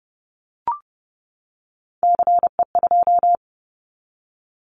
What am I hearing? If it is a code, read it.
Morse code, heard as a keyed tone, about 30 wpm, 700 Hz. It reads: CE2
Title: Callsign Prefixes for Top 100 DXCC - 30wpm